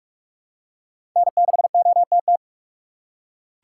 Morse code, heard as a keyed tone, about 44 wpm, 700 Hz. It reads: N6OTT
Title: USA Call Signs - 5 Characters - 4x Character Spacing - 45wpm